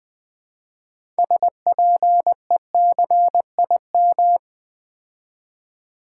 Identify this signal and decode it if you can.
Morse code, heard as a keyed tone, about 21 wpm, 700 Hz. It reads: SPECIM
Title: Build A Word - 3 Letters At A Time - 8-Letter Words - 20wpm